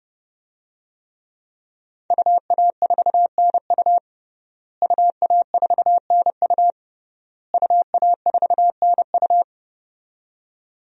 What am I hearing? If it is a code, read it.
Morse code, heard as a keyed tone, about 32 wpm, 700 Hz. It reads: UA4NU UA4NU UA4NU
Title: DXCC-100 Call Signs - 5 Characters - Repeated 3x - 30wpm